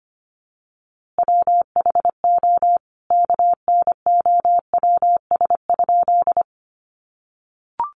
Morse code, heard as a keyed tone, about 25 wpm, 700 Hz. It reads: WHO KNOWS?
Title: A Sound Of Thunder 25wpm